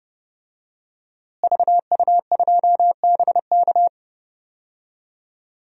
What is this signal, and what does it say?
Morse code, heard as a keyed tone, about 30 wpm, 700 Hz. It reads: VU2BK
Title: DXCC-100 Call Signs - 5 Characters - Repeated 3x - 30wpm